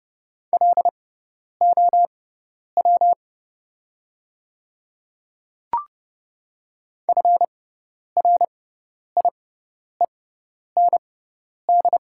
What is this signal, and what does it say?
Morse code, heard as a keyed tone, about 30 wpm, spaced as for 10 wpm, 700 Hz. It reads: LOW FRIEND
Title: 6x Character Spacing - Top 500 Words - 30wpm